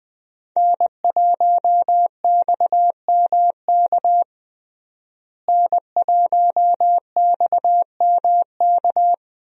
Morse code, heard as a keyed tone, about 20 wpm, 700 Hz. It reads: N1XMK N1XMK
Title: USA Call Signs - 5 Characters - Repeated 2x - 20wpm